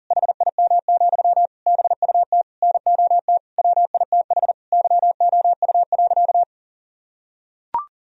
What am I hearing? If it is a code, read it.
Morse code, heard as a keyed tone, about 40 wpm, 700 Hz. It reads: HIM, BUT NOT WITH YOU.
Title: Sentences from Top 1000 Words - Review Part 4 - 40wpm